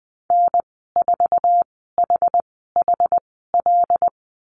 Morse code, heard as a keyed tone, about 20 wpm, spaced as for 14 wpm, 700 Hz. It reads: N4HHL
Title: 082 Lesson 14g - Number 2 - Call Signs 20wpm